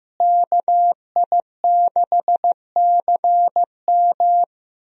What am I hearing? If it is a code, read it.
Morse code, heard as a keyed tone, about 15 wpm, 700 Hz. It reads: KI6CM
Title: USA Call Signs - 5 Characters - Repeated 3x - 15wpm